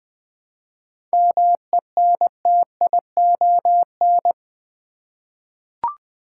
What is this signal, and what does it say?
Morse code, heard as a keyed tone, about 20 wpm, 690 Hz. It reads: MENTION
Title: 7 Letter Words - First 3 Letters Given - 20wpm